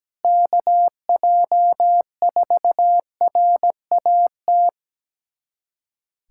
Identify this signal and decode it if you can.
Morse code, heard as a keyed tone, about 17 wpm, 700 Hz. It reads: KJ4RAT
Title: USA Callsign DE Callsign 17wpm